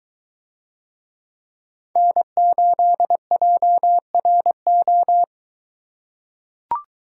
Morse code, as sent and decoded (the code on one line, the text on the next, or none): -. ---.. .--- .-. ---
N8JRO